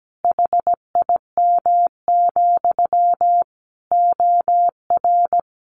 Morse code, heard as a keyed tone, about 17 wpm, 700 Hz. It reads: HIM, OR
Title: A Sound Of Thunder 17wpm